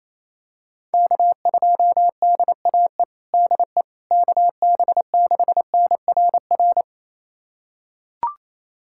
Morse code, heard as a keyed tone, about 27 wpm, 700 Hz. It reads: K2DAE DE KB6NRR
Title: USA Callsign DE Callsign 28wpm